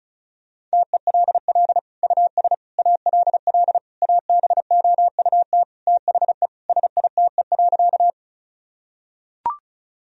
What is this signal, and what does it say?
Morse code, heard as a keyed tone, about 34 wpm, 700 Hz. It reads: TELL US ALL ABOUT THE SITE.
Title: Sentences from Top 1000 Words - Review Part 4 - 35wpm